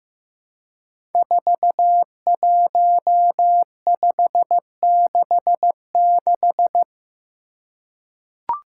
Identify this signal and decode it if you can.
Morse code, heard as a keyed tone, about 15 wpm, 700 Hz. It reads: 41566
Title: Five Challenging Letter-Number Combinations 15wpm